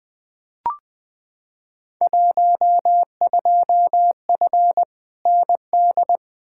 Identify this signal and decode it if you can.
Morse code, heard as a keyed tone, about 20 wpm, 700 Hz. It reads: 12F ND
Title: Field Day Exchanges Encore - 20wpm